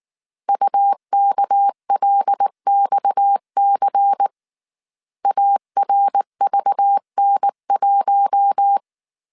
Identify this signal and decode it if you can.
Morse code, heard as a keyed tone, about 19 wpm, 800 Hz. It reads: UKLXC ARVN1